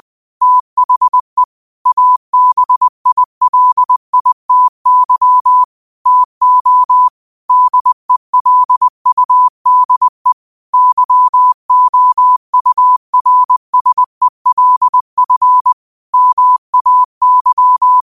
Morse code, heard as a keyed tone, about 20 wpm, 1000 Hz. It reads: THE ABILITY TO DELUDE YOURSELF MAY
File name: qotdfiles / 20wpm